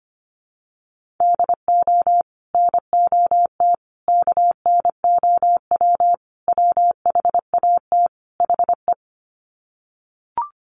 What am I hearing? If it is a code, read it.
Morse code, heard as a keyed tone, about 25 wpm, 700 Hz. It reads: DO NOT KNOW WHAT HE
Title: Sets of 5 Words Encore2 25wpm